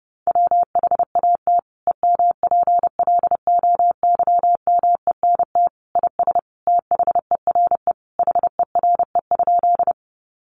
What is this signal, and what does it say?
Morse code, heard as a keyed tone, about 30 wpm, 700 Hz. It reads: WHAT EMPLOYMENT IS THERE HERE?
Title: Sentences from Top 1500 Words 30wpm